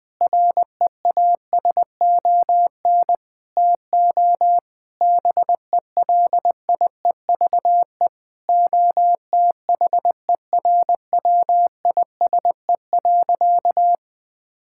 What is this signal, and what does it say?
Morse code, heard as a keyed tone, about 21 wpm, 700 Hz. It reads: REASON TO BELIEVE OTHERWISE.
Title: Sentences with 6-Word Phrases - 20wpm